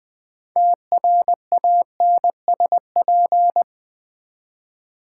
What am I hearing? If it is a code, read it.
Morse code, heard as a keyed tone, about 20 wpm, 700 Hz. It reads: TRANSP
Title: Build A Word - 3 Letters At A Time - 9-Letter Words - 20wpm